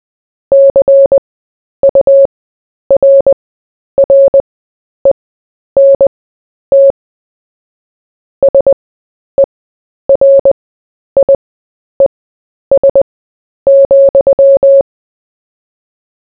Morse code, heard as a keyed tone, about 20 wpm, spaced as for 10 wpm, 550 Hz. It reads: CURRENT SERIES,
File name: cw / fists / Short-Story-4-10wpm